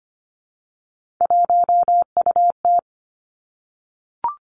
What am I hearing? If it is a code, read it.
Morse code, heard as a keyed tone, about 25 wpm, 700 Hz. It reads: 1UT
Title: Three Challenging Letter-Number Combinations 25wpm